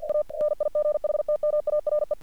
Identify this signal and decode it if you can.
Morse code, heard as a keyed tone, about 48 wpm, 600 Hz. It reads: A NIGHTMARE
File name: archived / CRSnet